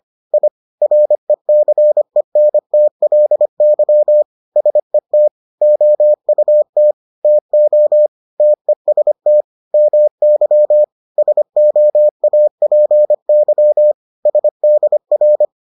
Morse code, heard as a keyed tone, about 25 wpm, 600 Hz. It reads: I RECENTLY SET OUT TO TEST MY SOAPY SDR